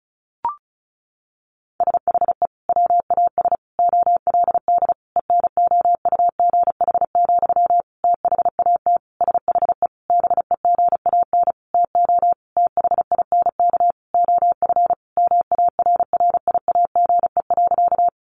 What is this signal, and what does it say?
Morse code, heard as a keyed tone, about 34 wpm, 700 Hz. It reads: SHE WAS OLD ENOUGH, THAT SHE BEGAN TO THINK OF MARRIAGE.
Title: Sentences from Top 800 Words 35wpm